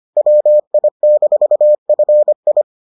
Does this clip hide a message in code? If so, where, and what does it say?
Yes: Morse code from 0.2 to 2.6 s, heard as a keyed tone, about 25 wpm, 600 Hz. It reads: WI-FI